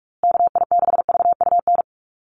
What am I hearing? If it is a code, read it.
Morse code, heard as a keyed tone, about 44 wpm, 700 Hz. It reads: KI6VUN
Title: USA Call Signs Repeated 2x 45wpm